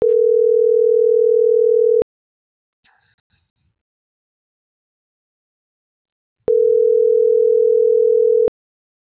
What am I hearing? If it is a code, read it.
Ringback tone (North America), 440 + 480 Hz.